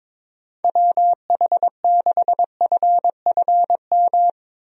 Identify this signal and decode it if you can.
Morse code, heard as a keyed tone, about 22 wpm, 700 Hz. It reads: WH6FFM
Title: USA Call Signs - 6 Characters - 22wpm